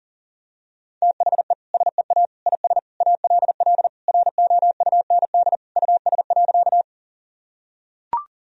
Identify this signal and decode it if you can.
Morse code, heard as a keyed tone, about 40 wpm, 700 Hz. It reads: THE SEA IS ALL ROUND US.
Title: Sentences from Top 1000 Words - Review Part 1 - 40wpm